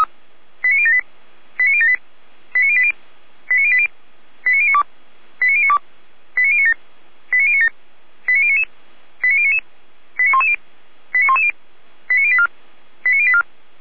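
Five-tone selective call (ZVEI), heard as quick runs of five tones, tones of about 70 ms. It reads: 89078 89078 89080 89080 89092 89092 89097 89097 89099 89099 89110 89110 89084 89084
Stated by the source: ZVEI five-tone selective call received by SDR.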